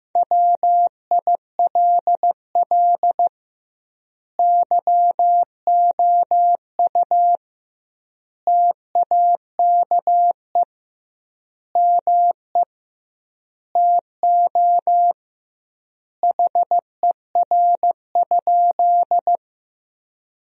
Morse code, heard as a keyed tone, about 15 wpm, 700 Hz. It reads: WILL YOU TAKE ME TO HER?
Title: 2x Word-Spacing - Sentences from Top 200 Words - 15wpm